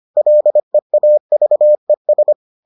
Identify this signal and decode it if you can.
Morse code, heard as a keyed tone, about 25 wpm, 600 Hz. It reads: LEAVES